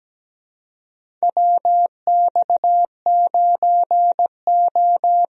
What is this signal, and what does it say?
Morse code, heard as a keyed tone, about 17 wpm, 700 Hz. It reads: WX9O
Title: USA Call Signs - 4 Characters - Repeated 2x - 17wpm